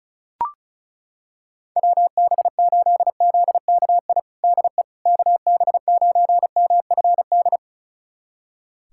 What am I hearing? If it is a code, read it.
Morse code, heard as a keyed tone, about 36 wpm, 700 Hz. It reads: WB8ZKI DE KB9MFD
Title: USA Callsign DE Callsign - 6 Characters - 35wpm